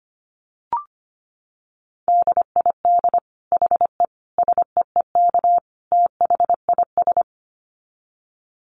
Morse code, heard as a keyed tone, about 25 wpm, 700 Hz. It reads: DID HE SEEK THIS
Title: Dit Challenge 25wpm